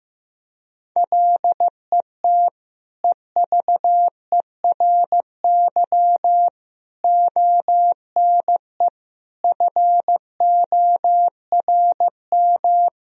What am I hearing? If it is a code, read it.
Morse code, heard as a keyed tone, about 15 wpm, 700 Hz. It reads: LET EVERY ONE FORM
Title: Sentences from Top 1000 Words - Review Part 1 - 15wpm